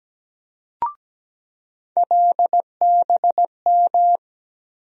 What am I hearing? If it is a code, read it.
Morse code, heard as a keyed tone, about 17 wpm, 700 Hz. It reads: LBM
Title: USA Call Sign Postfixes - 3 Letters 17wpm